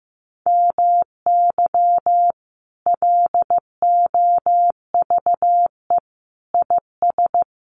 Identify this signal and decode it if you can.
Morse code, heard as a keyed tone, about 15 wpm, 700 Hz. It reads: MY LOVE IS
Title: Sentences from Top 1500 Words 15wpm